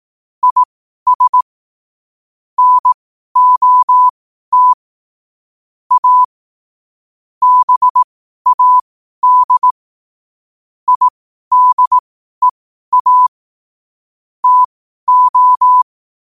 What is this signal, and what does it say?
Morse code, heard as a keyed tone, about 18 wpm, spaced as for 12 wpm, 1000 Hz. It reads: IS NOT A BAD IDEA TO
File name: qotdfiles / 12wpm